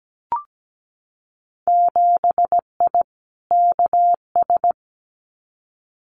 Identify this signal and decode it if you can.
Morse code, heard as a keyed tone, about 17 wpm, 700 Hz. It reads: 7I KS
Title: Winter Field Day Exchanges Encore 17wpm